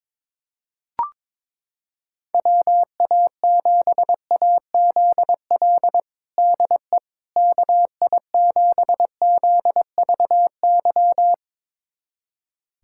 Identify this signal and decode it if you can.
Morse code, heard as a keyed tone, about 22 wpm, 700 Hz. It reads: WA7AZL DE KI7ZVY
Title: USA Callsign DE Callsign - 6 Characters - 22wpm